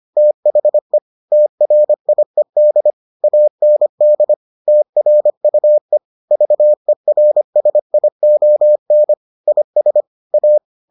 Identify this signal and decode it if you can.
Morse code, heard as a keyed tone, about 25 wpm, 600 Hz. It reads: THE TRIED AND TRUE VERSION IS A